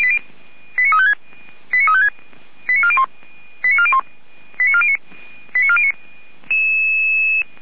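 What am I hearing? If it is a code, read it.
Five-tone selective call (ZVEI), heard as quick runs of five tones, tones of about 80 ms. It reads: 89356 89356 89401 89409 89409
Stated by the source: Five-tone ZVEI selective call received by SDR, from the feuerwehr-orlamuende.org fire brigade server.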